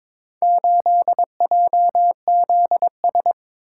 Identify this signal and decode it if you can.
Morse code, heard as a keyed tone, about 23 wpm, 700 Hz. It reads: 8JZS
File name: practice / 22wpm / Four Letter-Number Combinations - 22wpm